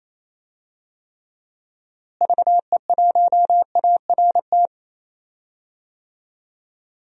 Morse code, heard as a keyed tone, about 28 wpm, 700 Hz. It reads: VE1ART
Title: DXCC-100 Call Signs - 6 Characters - Repeated 3x - 28wpm